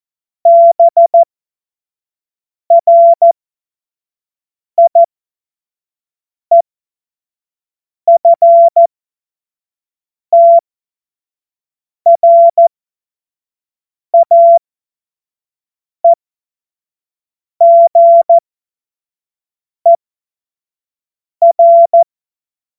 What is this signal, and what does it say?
Morse code, heard as a keyed tone, about 14 wpm, spaced as for 5 wpm, 680 Hz. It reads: BRIEFTRAEGER
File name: Morsekurs / Witze / Witz 20. Juli 2021, 25 F70